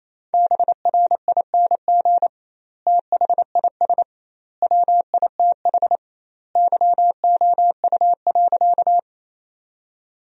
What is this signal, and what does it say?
Morse code, heard as a keyed tone, about 28 wpm, 700 Hz. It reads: BRING THIS WITH YOU.
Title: 2x Word-Spacing - Sentences from Top 500 Words - 28wpm